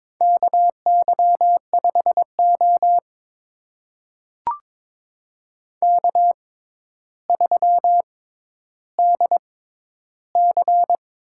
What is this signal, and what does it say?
Morse code, heard as a keyed tone, about 22 wpm, spaced as for 11 wpm, 700 Hz. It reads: KY5O K3DC
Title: USA Call Signs - 4 Characters - 6x Character Spacing - 22wpm